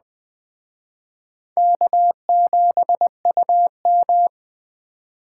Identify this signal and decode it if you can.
Morse code, heard as a keyed tone, about 20 wpm, 700 Hz. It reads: K7UM
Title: Speed Racing - USA 1x2 Call Signs 17wpm